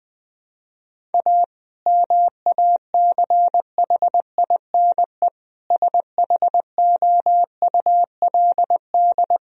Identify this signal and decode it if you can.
Morse code, heard as a keyed tone, about 20 wpm, 700 Hz. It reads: A MACHINE SHOULD
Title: Sentences from Top 1000 Words - Review Part 2 - 20wpm